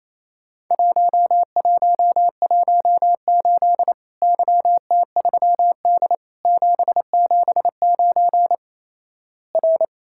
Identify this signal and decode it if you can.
Morse code, heard as a keyed tone, about 28 wpm, 700 Hz. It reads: 1118 YT3D 779 R
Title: Worked All Europe DX Contest - QTC Exchange - 28wpm